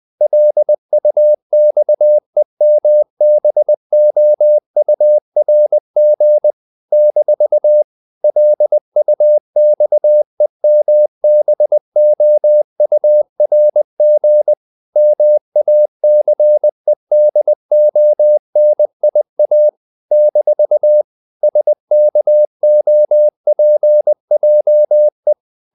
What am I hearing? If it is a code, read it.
Morse code, heard as a keyed tone, about 20 wpm, 600 Hz. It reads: LUXEMBOURG - LUXEMBOURG MACEDONIA - SKOPJE
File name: CW / Practice / NationalCapitals20